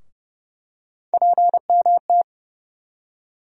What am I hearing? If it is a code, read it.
Morse code, heard as a keyed tone, about 30 wpm, 700 Hz. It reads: PMT
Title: USA Call Sign Postfixes - 3 Letters 30wpm